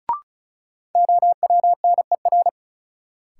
Morse code, heard as a keyed tone, about 36 wpm, 700 Hz. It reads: OWNER
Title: Sentences from Top 1600 Words - New Word Drill Alt - 35wpm